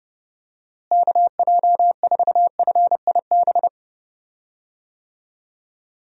Morse code, heard as a keyed tone, about 30 wpm, 700 Hz. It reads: KJ4FIB
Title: USA Call Signs - 6 Characters - Repeated 3x - 30wpm